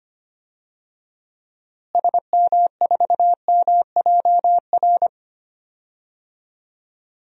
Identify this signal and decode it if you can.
Morse code, heard as a keyed tone, about 25 wpm, 700 Hz. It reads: SM4MJR
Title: DXCC-100 Call Signs - 6 Characters - Repeated 2x - 25wpm